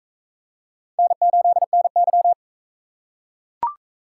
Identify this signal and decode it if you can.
Morse code, heard as a keyed tone, about 40 wpm, 700 Hz. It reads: N8NY